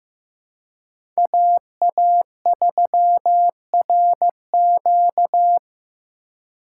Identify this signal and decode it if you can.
Morse code, heard as a keyed tone, about 15 wpm, 700 Hz. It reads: AA3RQ
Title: USA Call Signs - 3x Character Spacing - 15wpm